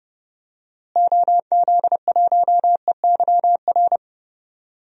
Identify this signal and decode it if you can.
Morse code, heard as a keyed tone, about 30 wpm, 700 Hz. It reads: OZ1EYR
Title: DXCC-100 Call Signs - 6 Characters - Repeated 2x - 30wpm